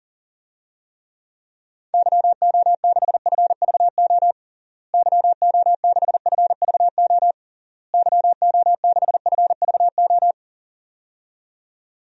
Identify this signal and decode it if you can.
Morse code, heard as a keyed tone, about 40 wpm, 700 Hz. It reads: YO6FVO YO6FVO YO6FVO
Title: DXCC-100 Call Signs - 6 Characters - Repeated 3x - 40wpm